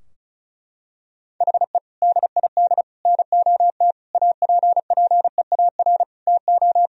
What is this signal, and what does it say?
Morse code, heard as a keyed tone, about 34 wpm, 700 Hz. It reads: HE DID NOT APPEAR TO
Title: Sentences from Top 1000 Words - Review Part 2 - 35wpm